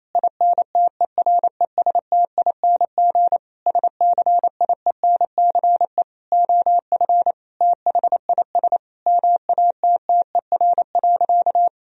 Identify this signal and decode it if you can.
Morse code, heard as a keyed tone, about 28 wpm, 700 Hz. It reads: INTERESTING SCIENCE OF THIS MATTER.
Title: Sentences from Top 1000 Words - Review Part 2 - 28wpm